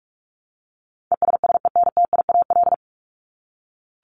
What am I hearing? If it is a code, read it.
Morse code, heard as a keyed tone, about 44 wpm, 700 Hz. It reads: ESSENTIAL
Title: LY Suffix - Top 100 Words with Base Words 45wpm